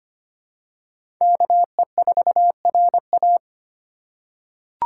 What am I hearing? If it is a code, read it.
Morse code, heard as a keyed tone, about 25 wpm, 700 Hz. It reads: KE4RA
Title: USA Call Signs - 5 Characters - 2x Character Spacing - 25wpm